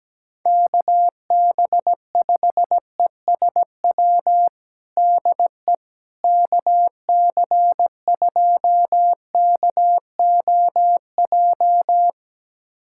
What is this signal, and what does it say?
Morse code, heard as a keyed tone, about 17 wpm, 700 Hz. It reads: KB5ESW DE KC2KOJ
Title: USA Callsign DE Callsign 17wpm